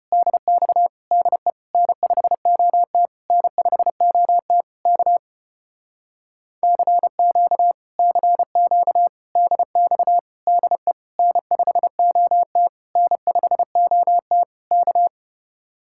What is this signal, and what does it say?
Morse code, heard as a keyed tone, about 33 wpm, 700 Hz. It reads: DX DE N5OT N5OT K CQ CQ DX DE N5OT N5OT K